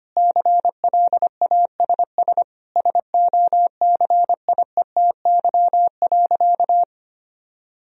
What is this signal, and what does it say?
Morse code, heard as a keyed tone, about 25 wpm, 700 Hz. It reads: CLASS SOCIETY.
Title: Sentences from Top 1000 Words - Review Part 4 - 25wpm